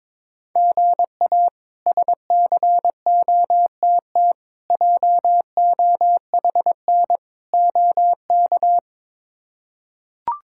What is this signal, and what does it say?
Morse code, heard as a keyed tone, about 22 wpm, 700 Hz. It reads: GA SCOTT JOHN OK
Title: K1USN SST QSO Practice - 22wpm